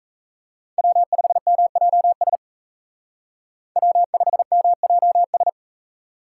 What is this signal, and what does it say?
Morse code, heard as a keyed tone, about 35 wpm, 700 Hz. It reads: W5MJS W5MJS